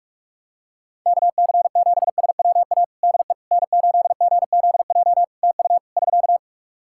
Morse code, heard as a keyed tone, about 46 wpm, 700 Hz. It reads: KC7SWA DE N8GZJ TU <SK>
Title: Practice QSO - Exchange 4 - Ending - 45wpm